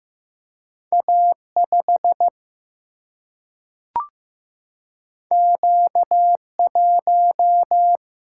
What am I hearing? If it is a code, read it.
Morse code, heard as a keyed tone, about 15 wpm, 700 Hz. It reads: A 5 Q 1
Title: Single-Letter Single-Number 15wpm